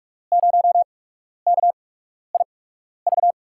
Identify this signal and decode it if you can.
Morse code, heard as a keyed tone, about 44 wpm, spaced as for 12 wpm, 700 Hz. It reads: 0KIV
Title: USA Call Signs - 8x Character Spacing - 45wpm